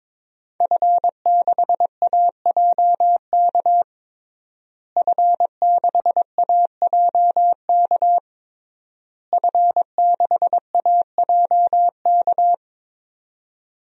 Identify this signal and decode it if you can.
Morse code, heard as a keyed tone, about 22 wpm, 700 Hz. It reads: F6AJK F6AJK F6AJK
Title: DXCC-100 Call Signs - 5 Characters - Repeated 3x - 22wpm